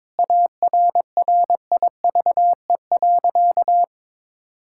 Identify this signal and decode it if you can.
Morse code, heard as a keyed tone, about 22 wpm, 700 Hz. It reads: ARRIVE.
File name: practice / 22wpm / Sentences with 4-Word Phrases - 22wpm